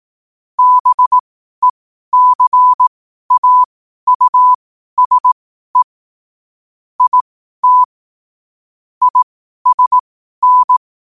Morse code, heard as a keyed tone, about 18 wpm, spaced as for 12 wpm, 1000 Hz. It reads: BECAUSE IT ISN
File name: qotdfiles / 12wpm